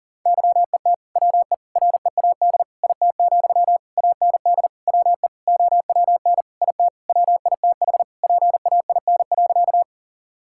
Morse code, heard as a keyed tone, about 40 wpm, 700 Hz. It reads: YET WE READ IT, AND WE OWN IT WITH PAIN.
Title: Sentences from Top 1600 Words 40wpm